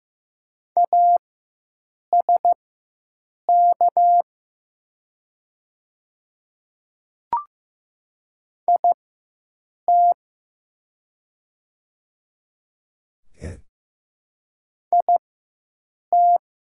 Morse code, heard as a keyed tone, about 15 wpm, spaced as for 7 wpm, 700 Hz. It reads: ASK IT IT
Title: 4x Character Spacing - Top 200 Words - 15wpm